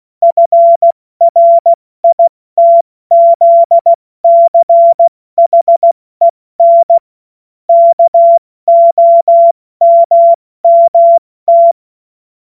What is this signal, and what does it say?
Morse code, heard as a keyed tone, about 16 wpm, 680 Hz. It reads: FRITZCHEN KOMMT
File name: Morsekurs / Witze / Witz 21. September 2021, 70 F80